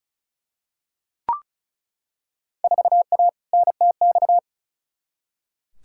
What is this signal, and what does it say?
Morse code, heard as a keyed tone, about 36 wpm, 700 Hz. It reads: 4A NTX
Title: Field Day Exchanges - 35wpm